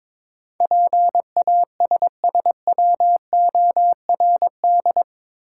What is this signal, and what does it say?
Morse code, heard as a keyed tone, about 22 wpm, 700 Hz. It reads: PASSWORD
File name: practice / 22wpm / Eight Letter Words - 22wpm